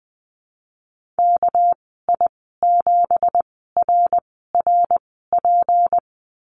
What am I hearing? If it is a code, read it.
Morse code, heard as a keyed tone, about 20 wpm, 700 Hz. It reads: K I 7 R R P
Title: 238 Lesson 38g - Letter q - Call Signs 20wpm